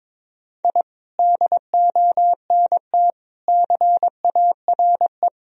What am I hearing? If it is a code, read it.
Morse code, heard as a keyed tone, about 22 wpm, 700 Hz. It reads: I DONT CARE
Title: Sentences from Top 1000 Words - Review Part 3 - 22wpm